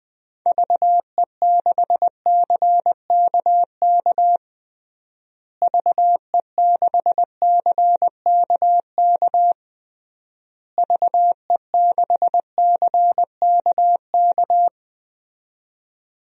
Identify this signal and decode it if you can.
Morse code, heard as a keyed tone, about 20 wpm, 700 Hz. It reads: VE6CKK VE6CKK VE6CKK
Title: DXCC-100 Call Signs - 6 Characters - Repeated 3x - 20wpm